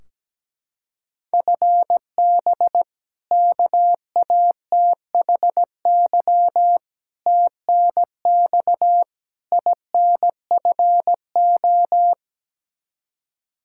Morse code, heard as a keyed tone, about 17 wpm, 700 Hz. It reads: FB KATHY TNX INFO
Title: Practice QSO - Exchange 3 - More Info - 17wpm